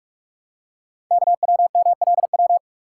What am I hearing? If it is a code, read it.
Morse code, heard as a keyed tone, about 44 wpm, 700 Hz. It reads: KWMLW
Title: Code Groups - 5 Letters - 45wpm